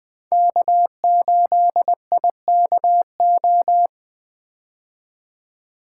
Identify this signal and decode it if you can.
Morse code, heard as a keyed tone, about 20 wpm, 700 Hz. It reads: K8IKO
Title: 2023 POTA Hunter Call Signs - 20wpm